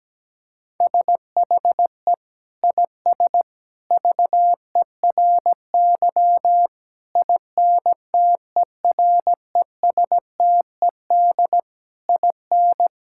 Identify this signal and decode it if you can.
Morse code, heard as a keyed tone, about 17 wpm, 700 Hz. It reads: SHE IS VERY INTERESTED IN
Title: Sentences from Top 1000 Words - Review Part 6 - 17wpm